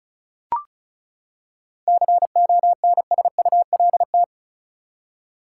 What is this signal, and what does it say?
Morse code, heard as a keyed tone, about 34 wpm, 700 Hz. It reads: CONSULT
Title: Seven Letter Words - 35wpm